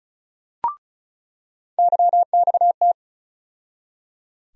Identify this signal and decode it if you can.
Morse code, heard as a keyed tone, about 36 wpm, 700 Hz. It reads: YXT
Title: USA Call Sign Postfixes - 3 Letters 35wpm